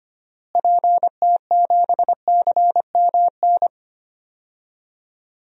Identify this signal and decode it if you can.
Morse code, heard as a keyed tone, about 25 wpm, 700 Hz. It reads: PT7CMN
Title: DXCC-100 Call Signs - 6 Characters - Repeated 2x - 25wpm